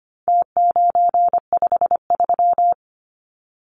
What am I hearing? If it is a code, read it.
Morse code, heard as a keyed tone, about 25 wpm, 700 Hz. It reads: T953
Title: Numbers - Contest Style - 4-Digit 25wpm